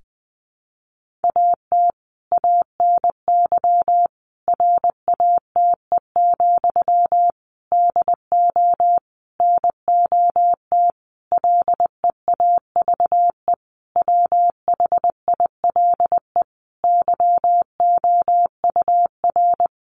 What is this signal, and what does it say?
Morse code, heard as a keyed tone, about 20 wpm, 700 Hz. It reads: AT ANY RATE, DO NOT LEAVE WHILE YOUR
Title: Sentences from Top 1500 Words 20wpm